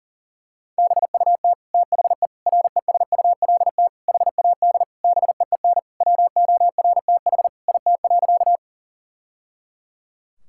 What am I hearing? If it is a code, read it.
Morse code, heard as a keyed tone, about 40 wpm, 700 Hz. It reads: BUT THE RESULT HAD BEEN WORTH IT.
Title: Sentences from Top 1000 Words - Review Part 5 - 40wpm